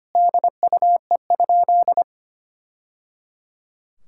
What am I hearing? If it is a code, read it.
Morse code, heard as a keyed tone, about 25 wpm, 700 Hz. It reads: DUE?
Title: Sentences from Top 1000 Words - Review Part 6 - 25wpm